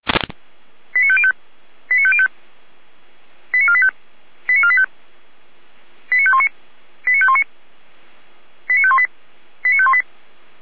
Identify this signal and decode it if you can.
Five-tone selective call (ZVEI), heard as quick runs of five tones, tones of about 70 ms. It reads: 89505 89505 89485 89485 89519 89519 89518 89518